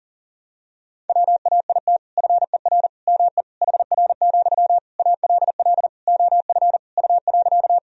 Morse code, heard as a keyed tone, about 40 wpm, 700 Hz. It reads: WAIT FER ME HR, ALL OF U.
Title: Sentences from Top 500 Words and Abbreviations 40wpm